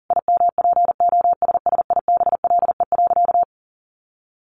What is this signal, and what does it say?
Morse code, heard as a keyed tone, about 40 wpm, 700 Hz. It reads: IMPOSSIBLE.
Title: Sentences from Top 1100 Words 40wpm